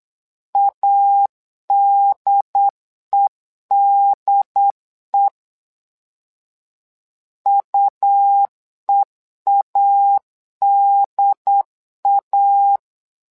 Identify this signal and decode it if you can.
Morse code, heard as a keyed tone, about 8 wpm, 800 Hz. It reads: ADEDE UEADA
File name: LEZIONI / 2 UHDN / 2 UHDN 8wpm2232